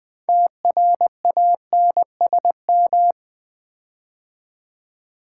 Morse code, heard as a keyed tone, about 20 wpm, 700 Hz. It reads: TRANSM
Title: Build A Word - 6 Letters At A Time - 12-Letter Words - 20wpm